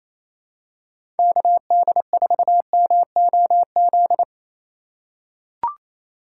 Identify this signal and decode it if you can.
Morse code, heard as a keyed tone, about 27 wpm, 700 Hz. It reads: KD4MOZ
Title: USA Call Signs - 6 Characters - 6x Character Spacing - 28wpm